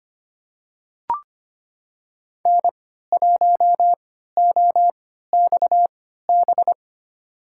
Morse code, heard as a keyed tone, about 25 wpm, spaced as for 14 wpm, 700 Hz. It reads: N1OXB
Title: USA Call Signs - 5 Characters - 3x Character Spacing - 25wpm